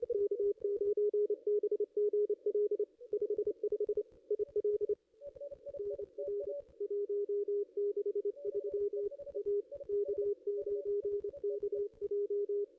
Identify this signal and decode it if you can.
Morse code, heard as a keyed tone, about 28 wpm, 410 Hz. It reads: K9BGL 55 IL NR 163 A K9KJ
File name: contests / ss-cw-k5tr-op-n5rz -0500